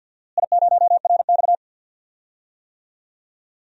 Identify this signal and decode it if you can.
Morse code, heard as a keyed tone, about 47 wpm, 700 Hz. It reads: I0RX
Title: DXCC-100 Call Signs - 4 Characters - Repeated 2x - 50wpm